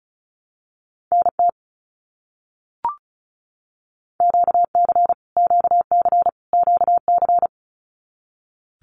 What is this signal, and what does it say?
Morse code, heard as a keyed tone, about 34 wpm, 700 Hz. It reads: NT QC QC QC
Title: Canadian Provinces and Territories - Repeated 3x - 35wpm